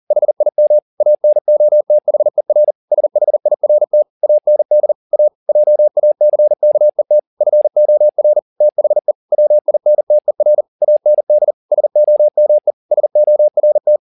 Morse code, heard as a keyed tone, about 40 wpm, 600 Hz. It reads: HIM ANOTHER SHIRT AND A JACKET FOR THE WINTER AND SOME SORT